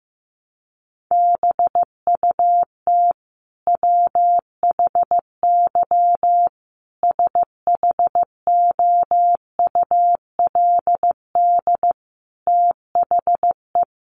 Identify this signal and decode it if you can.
Morse code, heard as a keyed tone, about 15 wpm, 700 Hz. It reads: BUT WHY SHOULD THE
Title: Sentences from Top 1100 Words 15wpm